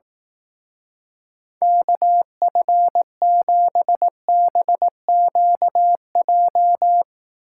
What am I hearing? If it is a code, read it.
Morse code, heard as a keyed tone, about 18 wpm, 700 Hz. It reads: KF7BQJ
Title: Speed Racing - USA 2x3 Call Signs 15wpm